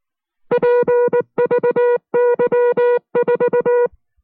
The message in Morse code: PVY4